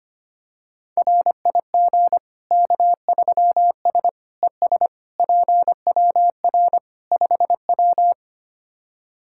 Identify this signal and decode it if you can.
Morse code, heard as a keyed tone, about 25 wpm, 700 Hz. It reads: RIG K3S ES PWR 5W
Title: K1USN SST QSO Practice 25wpm